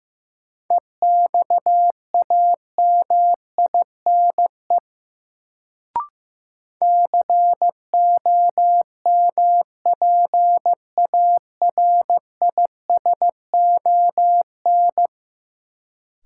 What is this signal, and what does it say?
Morse code, heard as a keyed tone, about 15 wpm, 700 Hz. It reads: EXAMINE COMPARISON
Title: Sentences from Top 1600 Words - New Word Drill Alt - 15wpm